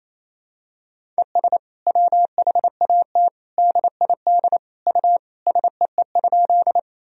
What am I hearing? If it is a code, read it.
Morse code, heard as a keyed tone, about 28 wpm, 700 Hz. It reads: ES WHAT DID U SEE?
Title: Sentences from Top 100 Words and Abbreviations 28wpm